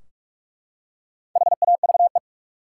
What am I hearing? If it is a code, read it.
Morse code, heard as a keyed tone, about 44 wpm, 700 Hz. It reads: HAVE
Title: Top 180 QSO Elements 45wpm